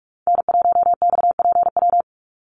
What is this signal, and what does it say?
Morse code, heard as a keyed tone, about 44 wpm, 700 Hz. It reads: N1XPW
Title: USA Call Signs 45wpm